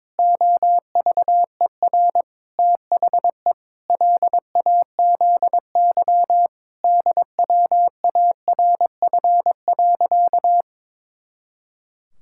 Morse code, heard as a keyed tone, about 22 wpm, 700 Hz. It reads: OVER THE LAZY DWARF.
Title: Pangram Challenge v2 - 22wpm